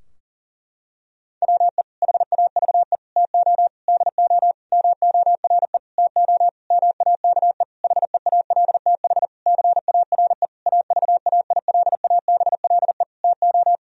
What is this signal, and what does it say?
Morse code, heard as a keyed tone, about 40 wpm, 700 Hz. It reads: WE HAVE TO DO MORE TO MAKE HEALTH CARE AVAILABLE TO
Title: Sentences from Top 1000 Words - Review Part 3 - 40wpm